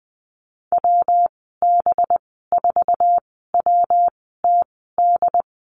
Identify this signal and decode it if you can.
Morse code, heard as a keyed tone, about 20 wpm, spaced as for 15 wpm, 700 Hz. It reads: WB4WTD
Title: 193 Lesson 31g - Letter v - Call Signs 20wpm